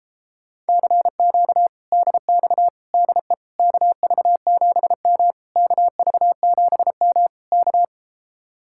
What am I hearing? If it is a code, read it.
Morse code, heard as a keyed tone, about 33 wpm, 700 Hz. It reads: CQ DX DE KV7M KV7M K